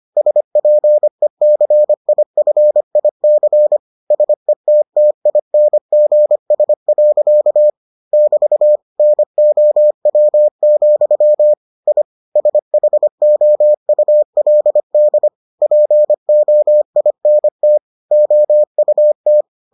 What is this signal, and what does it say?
Morse code, heard as a keyed tone, about 25 wpm, 600 Hz. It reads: SPECIFIC SETTINGS. <BT> NOW, I SHOULD POINT OUT